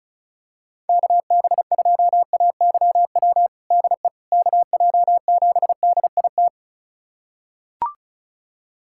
Morse code, heard as a keyed tone, about 34 wpm, 700 Hz. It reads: KB2AYW DE KJ7DIT
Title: USA Callsign DE Callsign - 6 Characters - 35wpm